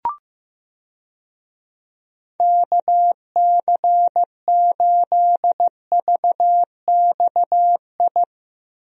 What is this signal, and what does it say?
Morse code, heard as a keyed tone, about 15 wpm, 700 Hz. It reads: KC8VXI
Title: USA Call Signs - 6 Characters - Repeated 3x - 15wpm